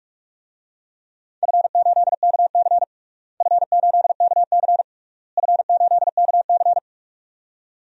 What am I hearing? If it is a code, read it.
Morse code, heard as a keyed tone, about 42 wpm, 700 Hz. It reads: F8KC F8KC F8KC
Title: DXCC-100 Call Signs - 4 Characters - Repeated 3x - 45wpm